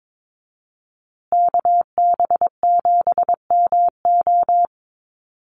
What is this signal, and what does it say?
Morse code, heard as a keyed tone, about 22 wpm, 700 Hz. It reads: KB7MO
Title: USA Call Signs Repeated 2x 22wpm